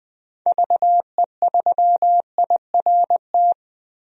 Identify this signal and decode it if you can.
Morse code, heard as a keyed tone, about 20 wpm, 700 Hz. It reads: VE3IRT
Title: DXCC-100 Call Signs - 6 Characters - 20wpm